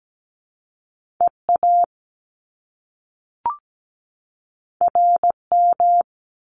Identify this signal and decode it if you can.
Morse code, heard as a keyed tone, about 17 wpm, 700 Hz. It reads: EA RM
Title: Top Two Letter Combinations in Words 17wpm